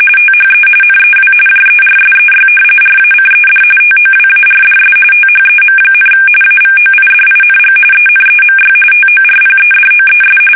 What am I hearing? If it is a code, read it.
RTTY, about 45.45 baud. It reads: THE NEW JERSEY RACED TOWARDS THEM, AVOIDING ENEMY TORPEDOES
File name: fsk850 / battleshipnj